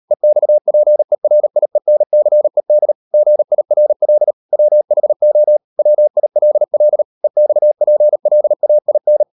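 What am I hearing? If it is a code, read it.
Morse code, heard as a keyed tone, about 38 wpm, 600 Hz. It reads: EXPERIENCED GIRL WHO WILL EXPLAIN
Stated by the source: Morse code practice at 38 WPM, sending Ken Follett's novel Never in English, from CW med Gnister.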